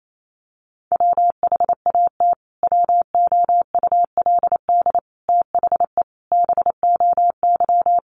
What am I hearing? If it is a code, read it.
Morse code, heard as a keyed tone, about 28 wpm, 700 Hz. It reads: WHAT WOULD THE BOY
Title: Sentences from Top 1300 Words 28wpm